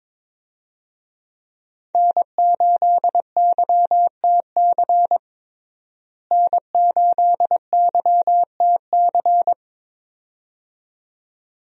Morse code, heard as a keyed tone, about 21 wpm, 700 Hz. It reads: N8YTC N8YTC
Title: USA Call Signs - 5 Characters - Repeated 2x - 22wpm